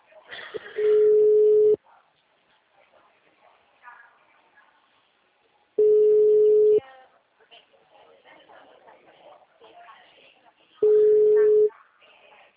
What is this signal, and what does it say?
Ringback tone (Europe, 425 Hz), with speech.